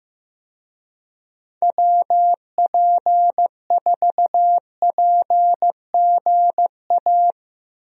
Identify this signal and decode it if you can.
Morse code, heard as a keyed tone, about 15 wpm, 700 Hz. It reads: WP4PGA
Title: USA Call Signs - 6 Characters - 15wpm